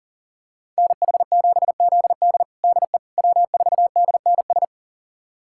Morse code, heard as a keyed tone, about 40 wpm, 700 Hz. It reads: NH7ZD DE W4DNS
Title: USA Callsign DE Callsign - 5 Characters - 40wpm